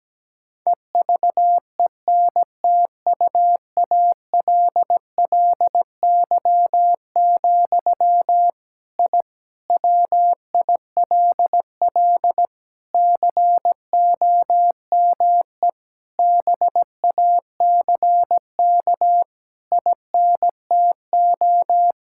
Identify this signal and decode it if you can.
Morse code, heard as a keyed tone, about 17 wpm, 700 Hz. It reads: EVENTUALLY, I WILL COME BACK INTO
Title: Sentences from Top 1000 Words - Review Part 6 - 17wpm